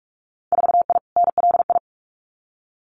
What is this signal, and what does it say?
Morse code, heard as a keyed tone, about 46 wpm, 700 Hz. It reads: 4I NLI
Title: Winter Field Day Exchanges Encore 45wpm